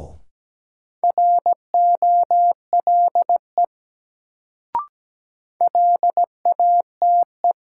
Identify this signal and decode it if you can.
Morse code, heard as a keyed tone, about 17 wpm, 700 Hz. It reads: ROLE LATE
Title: Sets of 3 Words - New Word Drill - 17wpm